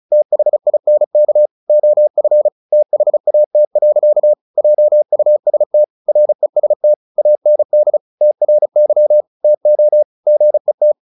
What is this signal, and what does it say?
Morse code, heard as a keyed tone, about 34 wpm, 600 Hz. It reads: THINK OF THAT. JUST REST AND TRY TO GET
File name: the old man and the sea 2 35wpm